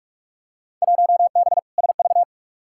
Morse code, heard as a keyed tone, about 43 wpm, 700 Hz. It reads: 1B SV
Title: Field Day Exchanges Encore - 45wpm